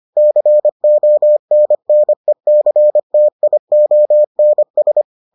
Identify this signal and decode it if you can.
Morse code, heard as a keyed tone, about 25 wpm, 600 Hz. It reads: CONNECTIONS